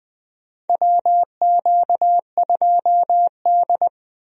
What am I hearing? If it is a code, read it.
Morse code, heard as a keyed tone, about 20 wpm, 700 Hz. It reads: WQ2D
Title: USA Call Signs - 4 Characters - 20wpm